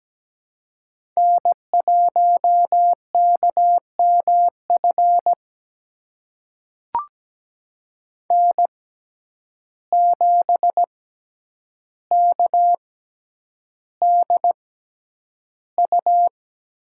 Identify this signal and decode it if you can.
Morse code, heard as a keyed tone, about 17 wpm, spaced as for 9 wpm, 700 Hz. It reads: N1KMF N7KDU
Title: USA Call Signs - 5 Characters - 6x Character Spacing - 17wpm